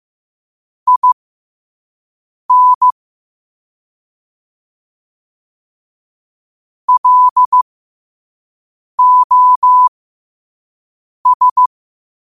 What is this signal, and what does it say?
Morse code, heard as a keyed tone, about 15 wpm, spaced as for 5 wpm, 1000 Hz. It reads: IN LOS
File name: qotdfiles / 5wpm